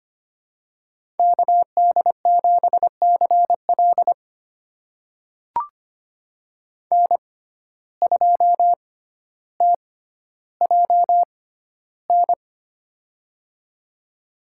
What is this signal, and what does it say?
Morse code, heard as a keyed tone, about 25 wpm, spaced as for 13 wpm, 700 Hz. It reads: KD7CL N2TJN
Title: USA Call Signs - 5 Characters - 6x Character Spacing - 25wpm